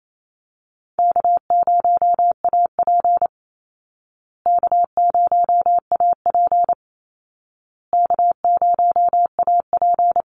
Morse code, heard as a keyed tone, about 28 wpm, 700 Hz. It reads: K0AP K0AP K0AP
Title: Call Signs Repeated 2x - 28wpm